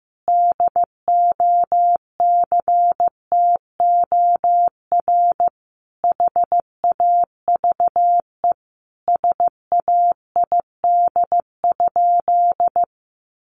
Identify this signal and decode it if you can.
Morse code, heard as a keyed tone, about 15 wpm, 700 Hz. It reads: DOCTOR HAVE SAID?
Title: Sentences from Top 1500 Words 15wpm